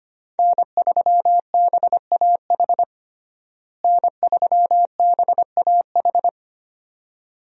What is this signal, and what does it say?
Morse code, heard as a keyed tone, about 25 wpm, 700 Hz. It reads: N3BAH N3BAH
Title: USA Call Signs - 5 Characters - Repeated 2x - 25wpm